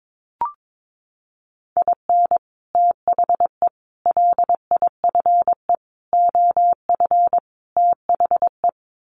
Morse code, heard as a keyed tone, about 22 wpm, 700 Hz. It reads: IN THE LIFE OF THE
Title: Sets of 5 Words Encore2 22wpm